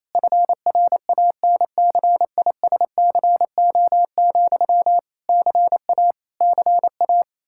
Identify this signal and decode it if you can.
Morse code, heard as a keyed tone, about 28 wpm, 700 Hz. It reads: FRANCISCO, CA CA
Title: Practice QSO - Exchange 1 - Beginning - 28wpm